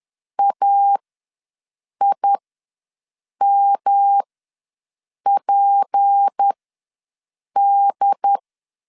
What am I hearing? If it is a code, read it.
Morse code, heard as a keyed tone, about 10 wpm, spaced as for 6 wpm, 800 Hz. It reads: AIMPD
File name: LEZIONI / 6 PKX0 / 6 P ws 10wpm 3751